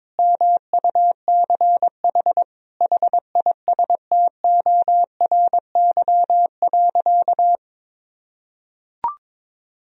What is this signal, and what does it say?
Morse code, heard as a keyed tone, about 22 wpm, 700 Hz. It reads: MUCH HISTORY.
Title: Sentences from Top 1000 Words - Review Part 2 - 22wpm